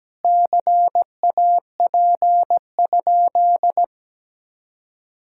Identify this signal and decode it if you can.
Morse code, heard as a keyed tone, about 17 wpm, 700 Hz. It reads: CAP?
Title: Sentences from Top 1600 Words 17wpm